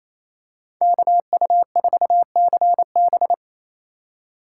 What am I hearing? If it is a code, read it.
Morse code, heard as a keyed tone, about 28 wpm, 700 Hz. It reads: KU4CB
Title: USA Call Signs - 5 Characters - 2x Character Spacing - 28wpm